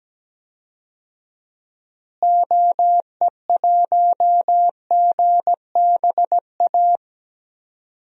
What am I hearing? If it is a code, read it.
Morse code, heard as a keyed tone, about 17 wpm, 700 Hz. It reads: OE1GBA
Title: DXCC-100 Call Signs - 6 Characters - Repeated 3x - 17wpm